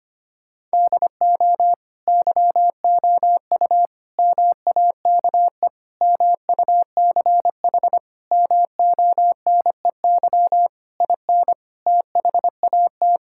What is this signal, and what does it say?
Morse code, heard as a keyed tone, about 25 wpm, 700 Hz. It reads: DO YOU MAKE MUCH MONEY IN THAT
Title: Sentences from Top 1000 Words - Review Part 6 - 25wpm